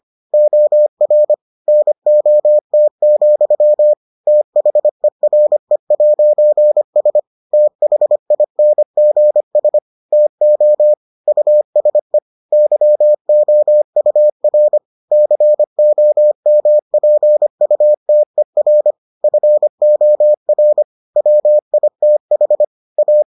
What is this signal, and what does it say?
Morse code, heard as a keyed tone, about 25 wpm, 600 Hz. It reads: OR NOT, THERE'S THINGS TO USE YOUR COMPUTER FOR WITH A